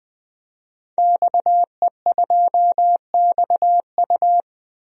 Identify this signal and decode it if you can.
Morse code, heard as a keyed tone, about 20 wpm, 700 Hz. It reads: XE2XU
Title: DXCC-100 Call Signs - 5 Characters - 20wpm